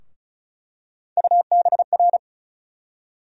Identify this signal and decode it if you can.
Morse code, heard as a keyed tone, about 36 wpm, 700 Hz. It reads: UBR
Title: USA Call Sign Postfixes - 3 Letters 35wpm